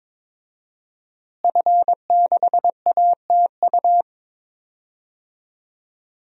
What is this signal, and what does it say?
Morse code, heard as a keyed tone, about 22 wpm, 700 Hz. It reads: F6ATU
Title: DXCC-100 Call Signs - 5 Characters - Repeated 3x - 22wpm